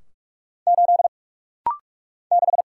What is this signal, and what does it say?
Morse code, heard as a keyed tone, about 46 wpm, 700 Hz. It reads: 8 6
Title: Single Letter-Number - Rapid-Fire 45wpm